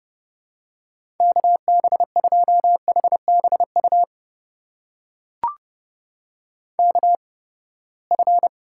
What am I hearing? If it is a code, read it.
Morse code, heard as a keyed tone, about 31 wpm, spaced as for 21 wpm, 700 Hz. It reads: KB2HBU KF
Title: USA Call Signs - 6 Characters - 8x Character Spacing - 30wpm